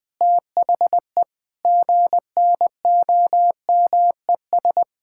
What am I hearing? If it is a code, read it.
Morse code, heard as a keyed tone, about 20 wpm, 700 Hz. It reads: THE GNOMES
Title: Pangram Challenge v2 - 20wpm